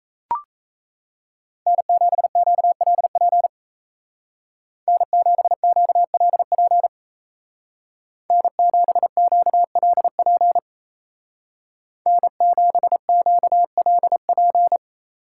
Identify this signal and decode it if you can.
Morse code, heard as a keyed tone, about 34 wpm, 700 Hz. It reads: N7QLP N7QLP N7QLP N7QLP
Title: Speed Racing - USA 1x3 Call Signs 28wpm